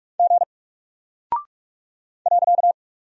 Morse code, heard as a keyed tone, about 44 wpm, 700 Hz. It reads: G .
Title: Single Character (Letter:Number:Punctuation:Prosign) Alt 45wpm